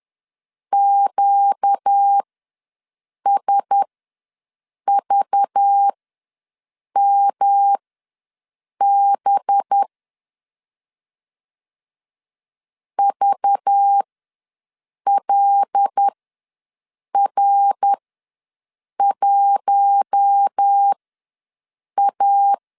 Morse code, heard as a keyed tone, about 10 wpm, spaced as for 6 wpm, 800 Hz. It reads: QSVMB VLR1A
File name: LEZIONI / 5 LBVC / 5 LBV ws 10wpm 3645